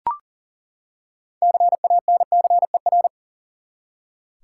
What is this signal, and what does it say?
Morse code, heard as a keyed tone, about 40 wpm, 700 Hz. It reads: CANCER
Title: Six Letter Words - 40wpm